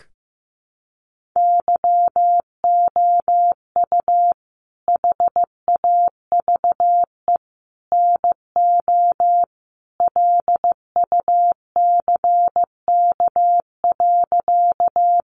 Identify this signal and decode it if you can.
Morse code, heard as a keyed tone, about 15 wpm, 700 Hz. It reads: YOU HAVE NO LUCK.
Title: Sentences from Top 1500 Words 15wpm